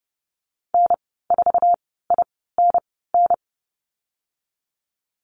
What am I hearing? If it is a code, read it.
Morse code, heard as a keyed tone, about 30 wpm, spaced as for 17 wpm, 700 Hz. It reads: N4INN
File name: Lesson 14g - Number 2 - Call Signs 30wpm